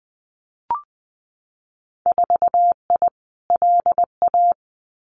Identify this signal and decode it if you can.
Morse code, heard as a keyed tone, about 20 wpm, 700 Hz. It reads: 4I LA
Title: Winter Field Day Exchanges Encore 20wpm